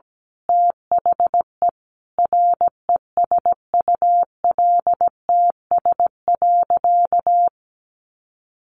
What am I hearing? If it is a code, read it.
Morse code, heard as a keyed tone, about 17 wpm, 700 Hz. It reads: THE RESULTS.
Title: A Sound Of Thunder 17wpm